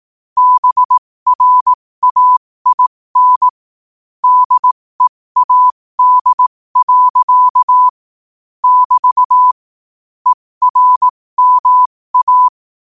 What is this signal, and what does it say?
Morse code, heard as a keyed tone, about 18 wpm, spaced as for 15 wpm, 1000 Hz. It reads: BRAIN DEAD. <BT> ERMA
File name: qotdfiles / 15wpm